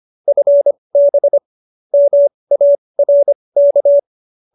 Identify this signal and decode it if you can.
Morse code, heard as a keyed tone, about 25 wpm, spaced as for 20 wpm, 590 Hz. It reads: FB MARK